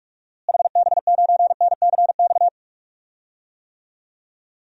Morse code, heard as a keyed tone, about 41 wpm, 700 Hz. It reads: HB9NCX
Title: DXCC-100 Call Signs - 6 Characters - Repeated 2x - 45wpm